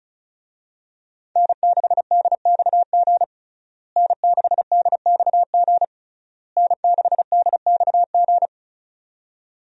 Morse code, heard as a keyed tone, about 36 wpm, 700 Hz. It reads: N6DXG N6DXG N6DXG
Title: USA Call Signs - 5 Characters - Repeated 3x - 35wpm